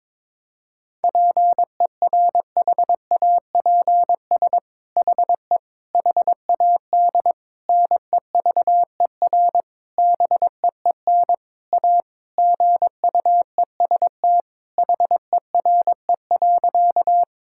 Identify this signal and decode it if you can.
Morse code, heard as a keyed tone, about 22 wpm, 700 Hz. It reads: PERHAPS HE HAD NEVER BEEN A GUEST HERE.
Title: Sentences from Top 1600 Words 22wpm